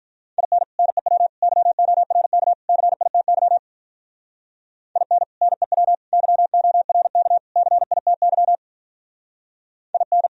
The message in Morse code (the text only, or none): IN NEW YORK CITY IN NEW YORK CITY IN